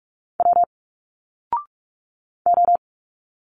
Morse code, heard as a keyed tone, about 46 wpm, 700 Hz. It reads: W O
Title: Single Letters 45wpm